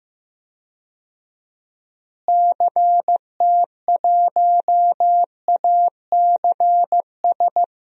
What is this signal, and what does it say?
Morse code, heard as a keyed tone, about 15 wpm, 700 Hz. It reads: CT1ACS
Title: DXCC-100 Call Signs - 6 Characters - Repeated 2x - 15wpm